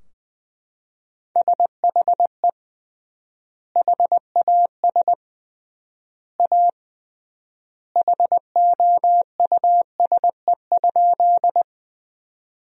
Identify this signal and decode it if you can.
Morse code, heard as a keyed tone, about 20 wpm, 700 Hz. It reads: SHE HAS A HOUSE?
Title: 3x Word-Spacing - Sentences from Top 200 Words - 20wpm